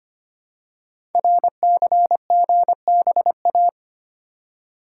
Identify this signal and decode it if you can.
Morse code, heard as a keyed tone, about 25 wpm, 700 Hz. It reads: RCGBA
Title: Code Groups - 5 Letters - 25wpm